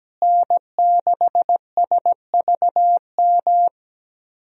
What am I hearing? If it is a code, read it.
Morse code, heard as a keyed tone, about 17 wpm, 700 Hz. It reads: N6SVM
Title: USA Call Signs - 5 Characters - Repeated 3x - 17wpm